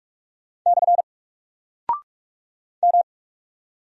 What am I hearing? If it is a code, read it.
Morse code, heard as a keyed tone, about 42 wpm, 700 Hz. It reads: / M
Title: Single Character (Letter:Number:Punctuation:Prosign) Alt 45wpm